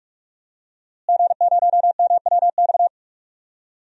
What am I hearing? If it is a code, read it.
Morse code, heard as a keyed tone, about 42 wpm, 700 Hz. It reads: G0MWX
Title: DXCC-100 Call Signs - 5 Characters - Repeated 2x - 45wpm